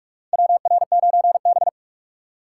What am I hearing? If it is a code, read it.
Morse code, heard as a keyed tone, about 43 wpm, 700 Hz. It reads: WR9B
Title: USA Call Signs - 4 Characters - 2x Character Spacing - 45wpm